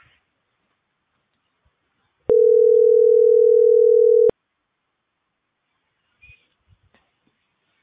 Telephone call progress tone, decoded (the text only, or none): ringback tone